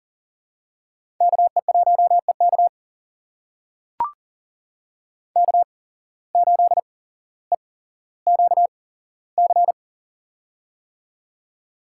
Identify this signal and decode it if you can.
Morse code, heard as a keyed tone, about 40 wpm, spaced as for 17 wpm, 700 Hz. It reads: KE1EK K8EQC
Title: USA Call Signs - 5 Characters - 8x Character Spacing - 40wpm